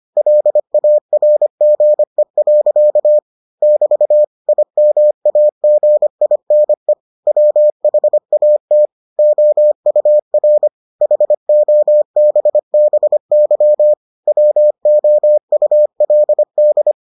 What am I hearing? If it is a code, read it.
Morse code, heard as a keyed tone, about 25 wpm, 600 Hz. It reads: LARGE. <BT> IMAGINE WHAT OUR HOBBY WOULD